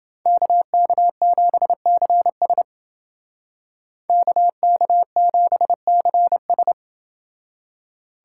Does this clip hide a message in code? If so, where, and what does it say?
Yes: Morse code from 0.3 to 6.7 s, heard as a keyed tone, about 28 wpm, 700 Hz. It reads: KK7CS KK7CS